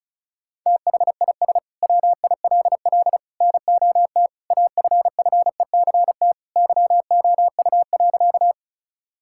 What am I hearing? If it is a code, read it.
Morse code, heard as a keyed tone, about 34 wpm, 700 Hz. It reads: THIS WILL NOT AFFECT YOU.
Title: Sentences from Top 1000 Words - Review Part 6 - 35wpm